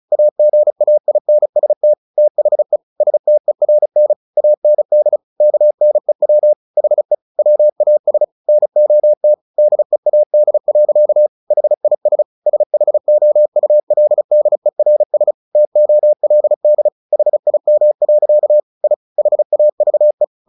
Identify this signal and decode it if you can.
Morse code, heard as a keyed tone, about 34 wpm, 600 Hz. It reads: AGAINST THE STERN AND KNEW HE WAS NOT DEAD. HIS SHOULDERS TOLD HIM. I HAVE
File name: the old man and the sea 2 35wpm